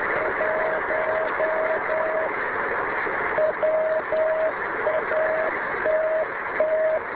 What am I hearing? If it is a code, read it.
Morse code, heard as a keyed tone, about 10 wpm, 630 Hz, with noise. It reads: 1 WATT